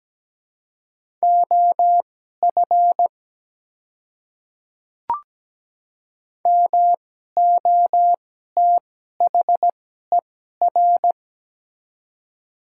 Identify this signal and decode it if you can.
Morse code, heard as a keyed tone, about 17 wpm, spaced as for 12 wpm, 700 Hz. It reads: OF MOTHER
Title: 2x Character Spacing - Top 200 Words - 17wpm